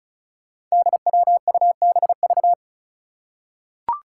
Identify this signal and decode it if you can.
Morse code, heard as a keyed tone, about 36 wpm, 700 Hz. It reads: DWUBV
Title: Five Challenging Letter-Number Combinations 35wpm